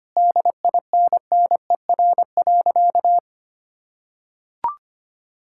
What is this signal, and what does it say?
Morse code, heard as a keyed tone, about 25 wpm, 700 Hz. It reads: DINNER.
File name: practice / 25wpm / Difficult Sentences with Hints - 25wpm